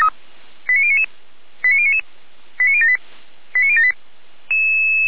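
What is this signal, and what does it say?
Five-tone selective call (ZVEI), heard as quick runs of five tones, tones of about 70 ms. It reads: 89099 89099 89078 89078
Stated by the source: ZVEI five-tone call received by SDR.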